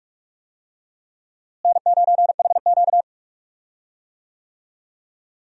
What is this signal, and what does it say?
Morse code, heard as a keyed tone, about 44 wpm, 700 Hz. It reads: N9HQ
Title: USA Call Signs - 4 Characters - Repeated 3x - 45wpm